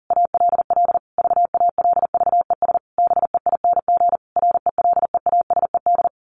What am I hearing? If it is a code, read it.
Morse code, heard as a keyed tone, about 40 wpm, 700 Hz. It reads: ALL VALVES BEING RELEASED
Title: A Sound Of Thunder 40wpm